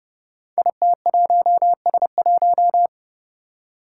Morse code, heard as a keyed tone, about 30 wpm, 700 Hz. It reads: IT1S1
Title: Five Challenging Letter-Number Combinations 30wpm